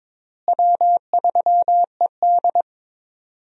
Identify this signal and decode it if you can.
Morse code, heard as a keyed tone, about 22 wpm, 700 Hz. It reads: W3ED
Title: USA Call Signs - 4 Characters - Repeated 2x - 22wpm